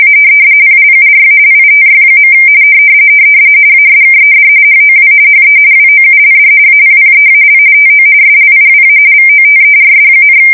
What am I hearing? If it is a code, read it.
RTTY, about 45.45 baud. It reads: A PLURALITY. ASKED HOW HE WOULD APPEAL TO SLIWA VOTERS,
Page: 170 Shift 2125/2295